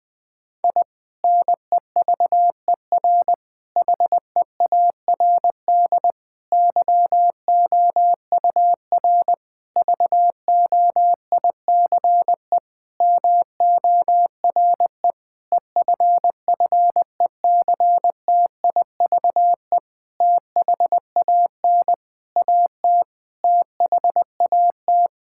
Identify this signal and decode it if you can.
Morse code, heard as a keyed tone, about 21 wpm, 700 Hz. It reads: I NEVER HEARD YOUR VOICE MORE EFFECTIVE THAN AT THAT
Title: Sentences from Top 1000 Words - Review Part 6 - 20wpm